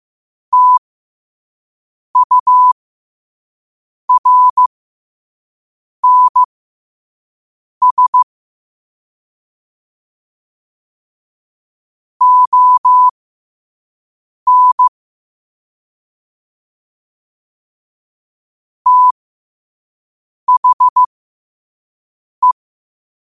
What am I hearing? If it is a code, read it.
Morse code, heard as a keyed tone, about 15 wpm, spaced as for 5 wpm, 1000 Hz. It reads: TURNS ON THE
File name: qotdfiles / 5wpm